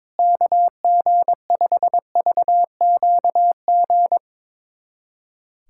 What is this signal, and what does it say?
Morse code, heard as a keyed tone, about 22 wpm, 700 Hz. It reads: KG5VQG
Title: USA Callsign DE Callsign 22wpm